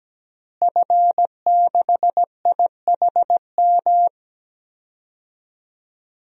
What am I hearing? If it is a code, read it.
Morse code, heard as a keyed tone, about 17 wpm, 700 Hz. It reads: F6IHM
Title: DXCC-100 Call Signs - 5 Characters - Repeated 3x - 17wpm